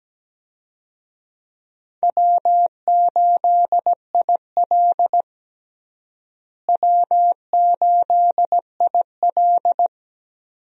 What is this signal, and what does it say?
Morse code, heard as a keyed tone, about 17 wpm, 700 Hz. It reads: W8IL W8IL
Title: USA Call Signs - 4 Characters - Repeated 2x - 17wpm